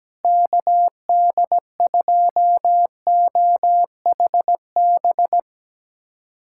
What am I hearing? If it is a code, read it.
Morse code, heard as a keyed tone, about 17 wpm, 700 Hz. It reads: KD2OHB
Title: USA Callsign DE Callsign - 6 Characters - 17wpm